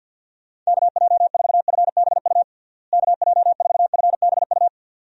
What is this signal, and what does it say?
Morse code, heard as a keyed tone, about 52 wpm, 700 Hz. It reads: KJ4FBU KJ4FBU
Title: USA Call Signs - 6 Characters - Repeated 2x - 50wpm